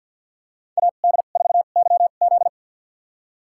Morse code, heard as a keyed tone, about 52 wpm, spaced as for 36 wpm, 700 Hz. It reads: AD4YZ
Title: USA Call Signs - 5 Characters - 2x Character Spacing - 50wpm